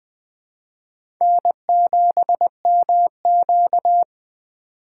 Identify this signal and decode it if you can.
Morse code, heard as a keyed tone, about 20 wpm, 700 Hz. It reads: N7MQ
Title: USA Call Signs - 4 Characters - 2x Character Spacing - 20wpm